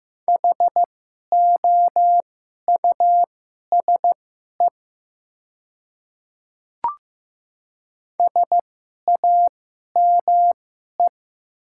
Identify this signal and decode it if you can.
Morse code, heard as a keyed tone, about 15 wpm, spaced as for 11 wpm, 700 Hz. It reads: HOUSE SAME
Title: 2x Character Spacing - Top 200 Words - 15wpm